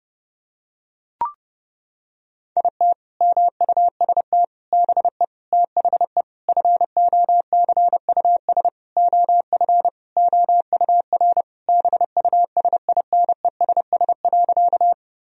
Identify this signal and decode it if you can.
Morse code, heard as a keyed tone, about 32 wpm, 700 Hz. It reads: IT MUST BE THE FOCUS OF OUR BUSINESS.
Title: Sentences from Top 1000 Words - Review Part 4 - 30wpm